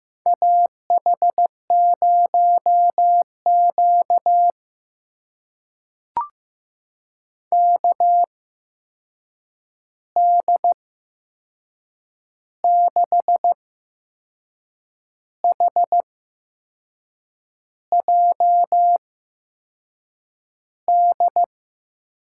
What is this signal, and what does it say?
Morse code, heard as a keyed tone, about 15 wpm, spaced as for 6 wpm, 700 Hz. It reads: AH0Q KD6HJD
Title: USA Call Signs - 8x Character Spacing - 15wpm